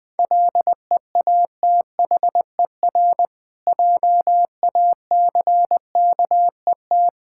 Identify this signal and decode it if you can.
Morse code, heard as a keyed tone, about 21 wpm, 700 Hz. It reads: LEATHER JACKET
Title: Difficult Words with Hints - 20wpm